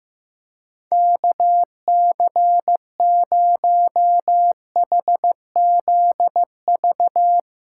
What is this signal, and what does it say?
Morse code, heard as a keyed tone, about 15 wpm, 700 Hz. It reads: KC0HZV
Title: USA Callsign DE Callsign - 6 Characters - 15wpm